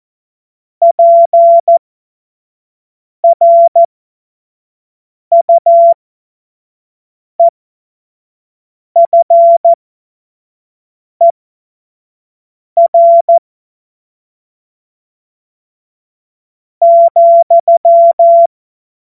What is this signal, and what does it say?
Morse code, heard as a keyed tone, about 14 wpm, spaced as for 4 wpm, 680 Hz. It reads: PRUEFER,
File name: Morsekurs / Witze / Witz 17. Mai 2022 25 F70